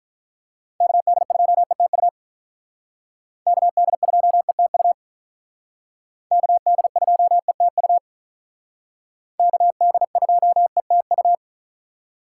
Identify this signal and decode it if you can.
Morse code, heard as a keyed tone, about 45 wpm, 700 Hz. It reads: KD2ETU KD2ETU KD2ETU KD2ETU